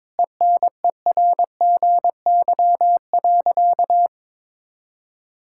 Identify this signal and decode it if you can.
Morse code, heard as a keyed tone, about 22 wpm, 700 Hz. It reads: ENERGY.
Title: Sentences from Top 1000 Words - Review Part 6 - 22wpm